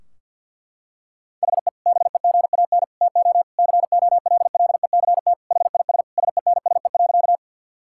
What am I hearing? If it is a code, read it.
Morse code, heard as a keyed tone, about 49 wpm, 700 Hz. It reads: HE BEGAN TO COLLECT HIS SENSE.
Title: Sentences from Top 1600 Words 50wpm